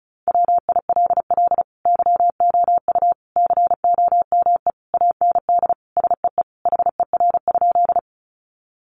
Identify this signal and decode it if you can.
Morse code, heard as a keyed tone, about 34 wpm, 700 Hz. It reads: WILL YOU COME AND SEE HER?
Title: Sentences from Top 100 Words 35wpm